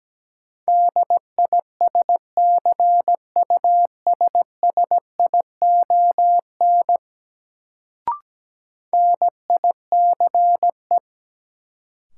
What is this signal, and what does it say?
Morse code, heard as a keyed tone, about 17 wpm, 700 Hz. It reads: DISCUSSION NICE
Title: Sentences from Top 1100 Words - New Word Drill Alt - 17wpm